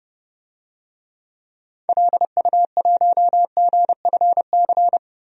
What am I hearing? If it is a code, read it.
Morse code, heard as a keyed tone, about 30 wpm, 700 Hz. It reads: LU1GFC
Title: DXCC-100 Call Signs - 6 Characters - 30wpm